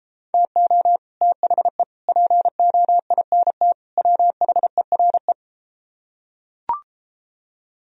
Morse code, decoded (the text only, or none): TO THE POINT WHERE